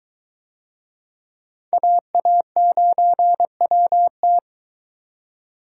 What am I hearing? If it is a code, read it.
Morse code, heard as a keyed tone, about 22 wpm, 700 Hz. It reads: AA9WT